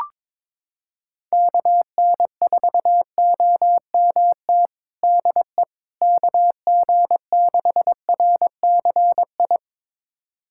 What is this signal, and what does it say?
Morse code, heard as a keyed tone, about 22 wpm, 700 Hz. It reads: KN4OMT DE KG6RCI
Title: USA Callsign DE Callsign - 6 Characters - 22wpm